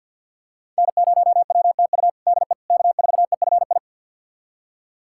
Morse code, heard as a keyed tone, about 52 wpm, 700 Hz. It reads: N0WTU DE K4EFI
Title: USA Callsign DE Callsign - 5 Characters - 50wpm